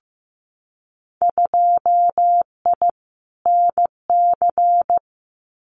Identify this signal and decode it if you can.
Morse code, heard as a keyed tone, about 15 wpm, 700 Hz. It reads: 2I NC
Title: Winter Field Day Exchanges 15wpm